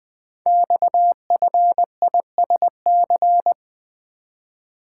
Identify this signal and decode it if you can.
Morse code, heard as a keyed tone, about 20 wpm, 700 Hz. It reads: XFISC
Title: Code Groups - 5 Letters - 20wpm